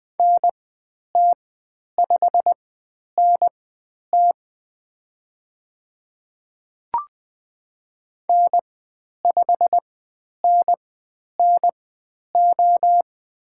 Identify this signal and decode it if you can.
Morse code, heard as a keyed tone, about 20 wpm, spaced as for 10 wpm, 700 Hz. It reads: NT5NT N5NNO
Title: 109 Lesson 18g - Letter m - Call Signs 20wpm